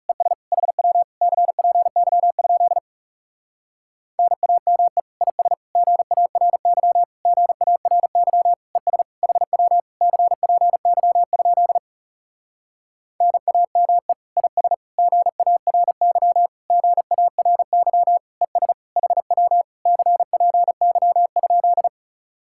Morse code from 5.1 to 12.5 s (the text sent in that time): IS GARY GARY ES HW CPY?